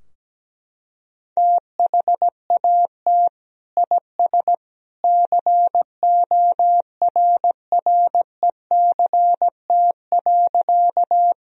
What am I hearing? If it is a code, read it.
Morse code, heard as a keyed tone, about 17 wpm, 700 Hz. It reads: THAT IS CORRECT.
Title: Sentences from Top 1000 Words - Review Part 2 - 17wpm